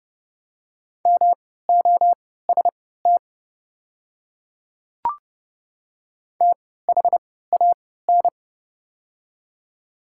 Morse code, heard as a keyed tone, about 30 wpm, spaced as for 17 wpm, 700 Hz. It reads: MOST THAN
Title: 3x Character Spacing - Top 100 Words - 30wpm